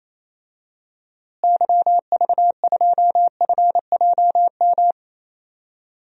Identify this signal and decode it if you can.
Morse code, heard as a keyed tone, about 28 wpm, 700 Hz. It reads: YV2FJM
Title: DXCC-100 Call Signs - 6 Characters - Repeated 2x - 28wpm